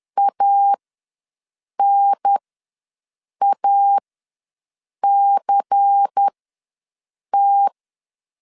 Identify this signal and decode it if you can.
Morse code, heard as a keyed tone, about 10 wpm, spaced as for 6 wpm, 800 Hz. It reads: ANACT
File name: LEZIONI / 6 PKX0 / 6 P ws 10wpm 3751